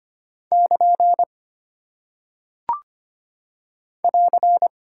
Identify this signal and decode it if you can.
Morse code, heard as a keyed tone, about 25 wpm, 700 Hz. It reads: <KN> <AR>
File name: practice / 25wpm / Single Punctuation-Prosign - 25wpm